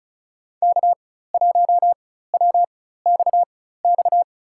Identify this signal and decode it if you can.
Morse code, heard as a keyed tone, about 36 wpm, spaced as for 16 wpm, 700 Hz. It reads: K1WXX
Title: USA Call Signs - 4x Character Spacing - 35wpm